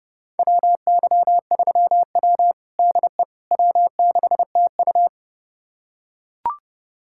Morse code, heard as a keyed tone, about 32 wpm, 700 Hz. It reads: WY3W DE W6TU
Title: USA Callsign DE Callsign - 4 Characters - 30wpm